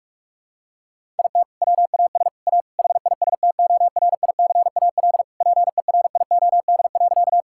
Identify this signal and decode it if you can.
Morse code, heard as a keyed tone, about 46 wpm, 690 Hz. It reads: IT WAS A HISTORICAL PERIOD.
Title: Sentences from Top 1000 Words - Review Part 5 - 45wpm